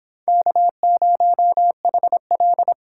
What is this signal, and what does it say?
Morse code, heard as a keyed tone, about 25 wpm, 700 Hz. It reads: K0HL
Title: Speed Racing - USA 1x2 Call Signs 22wpm